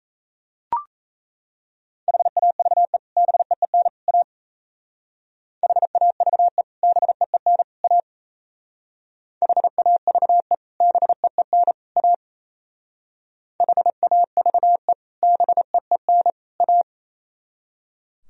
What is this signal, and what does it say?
Morse code, heard as a keyed tone, about 34 wpm, 700 Hz. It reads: HAVE BEEN A HAVE BEEN A HAVE BEEN A HAVE BEEN A
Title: Speed-Racing - Sets of 3 Words 28wpm